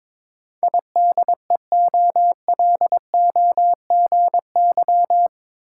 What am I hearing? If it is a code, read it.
Morse code, heard as a keyed tone, about 22 wpm, 700 Hz. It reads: IDEOLOGY